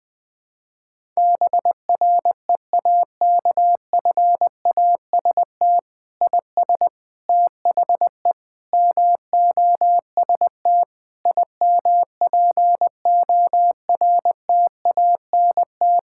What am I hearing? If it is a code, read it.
Morse code, heard as a keyed tone, about 21 wpm, 700 Hz. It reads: BREAKFAST IS THE MOST IMPORTANT
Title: Sentences with 9-Word Phrases - Spoken Beginning - 20wpm